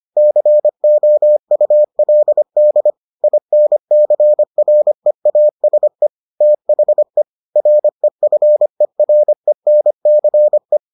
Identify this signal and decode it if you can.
Morse code, heard as a keyed tone, about 25 wpm, 600 Hz. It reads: COULD INCREASE THE REFERENCE